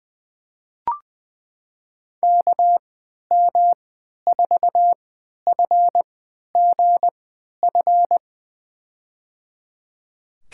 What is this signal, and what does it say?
Morse code, heard as a keyed tone, about 20 wpm, spaced as for 11 wpm, 700 Hz. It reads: KM4FGF
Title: USA Call Signs - 6 Characters - 3x Character Spacing - 20wpm